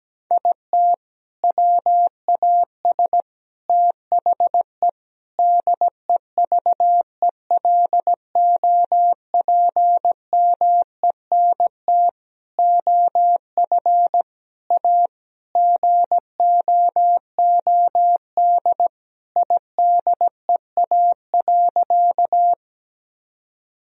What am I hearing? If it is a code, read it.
Morse code, heard as a keyed tone, about 17 wpm, 700 Hz. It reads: IT WAS THE DEVELOPMENT OF A GOOD IDEA.
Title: Sentences from Top 1000 Words - Review Part 4 - 17wpm